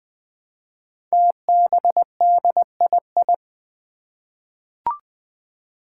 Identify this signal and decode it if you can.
Morse code, heard as a keyed tone, about 20 wpm, 700 Hz. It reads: TBDII
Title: Five Challenging Letter-Number Combinations 20wpm